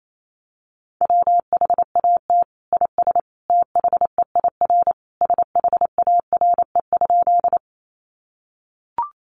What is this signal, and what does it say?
Morse code, heard as a keyed tone, about 28 wpm, 700 Hz. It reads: WHAT IS THEIR SHARE?
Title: Sentences from Top 900 Words 28wpm